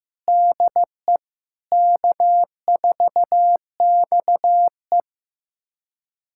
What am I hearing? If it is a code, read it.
Morse code, heard as a keyed tone, about 15 wpm, 700 Hz. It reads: DE K4XE
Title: USA Callsign DE Callsign 15wpm